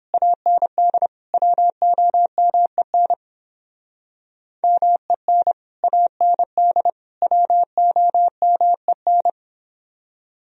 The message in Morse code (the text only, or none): AND WOMEN MEN AND WOMEN